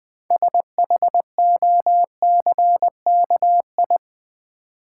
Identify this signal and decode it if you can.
Morse code, heard as a keyed tone, about 20 wpm, 700 Hz. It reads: SHOCKI
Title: Build A Word - 2 Letters At A Time - 8-Letter Words - 20wpm